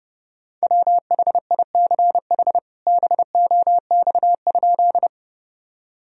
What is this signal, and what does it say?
Morse code, heard as a keyed tone, about 32 wpm, 700 Hz. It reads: WHICH BOX?
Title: Sentences from Top 1000 Words - Review Part 2 - 30wpm